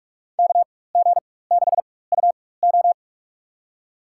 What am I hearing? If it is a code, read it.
Morse code, heard as a keyed tone, about 47 wpm, spaced as for 21 wpm, 700 Hz. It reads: KG6UO
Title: USA Call Signs - 5 Characters - 4x Character Spacing - 45wpm